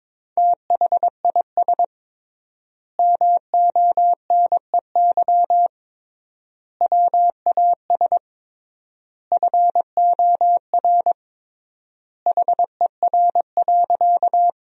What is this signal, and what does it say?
Morse code, heard as a keyed tone, about 22 wpm, 700 Hz. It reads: THIS MONEY WAS FOR HER.
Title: 3x Word-Spacing - Sentences from Top 400 Words - 22wpm